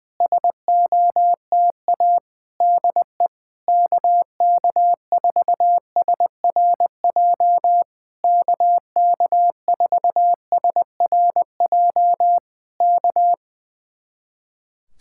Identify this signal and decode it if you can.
Morse code, heard as a keyed tone, about 20 wpm, 700 Hz. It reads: SOTA DE KK4SRJ KK4SRJ K
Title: Speed-Racing Common QSO Phrases Encore - 20wpm